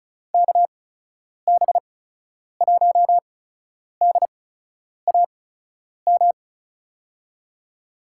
Morse code, heard as a keyed tone, about 36 wpm, spaced as for 10 wpm, 700 Hz. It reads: KB1DAM
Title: USA Call Signs - 6 Characters - 8x Character Spacing - 35wpm